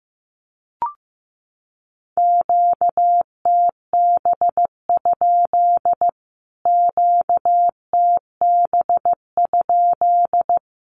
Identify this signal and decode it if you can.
Morse code, heard as a keyed tone, about 15 wpm, 700 Hz. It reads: QTB? QTB?
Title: Q-Codes Repeated 3x 15wpm